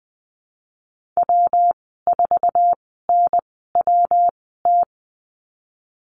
Morse code, heard as a keyed tone, about 20 wpm, spaced as for 15 wpm, 700 Hz. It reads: W4NWT
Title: 167 Lesson 27g - Number 7 - Call Signs 20wpm